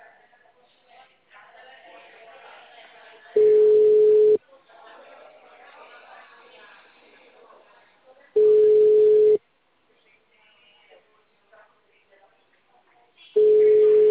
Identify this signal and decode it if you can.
Ringback tone (Europe, 425 Hz), with speech.